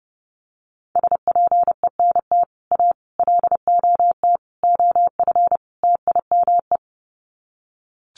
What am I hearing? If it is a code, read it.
Morse code, heard as a keyed tone, about 30 wpm, 700 Hz. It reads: SPENT A LOT OF TIME
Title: Sets of 5 Words 30wpm